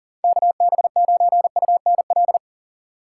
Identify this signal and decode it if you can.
Morse code, heard as a keyed tone, about 40 wpm, 700 Hz. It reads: KB9UNL
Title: USA Call Signs - 6 Characters - 2x Character Spacing - 40wpm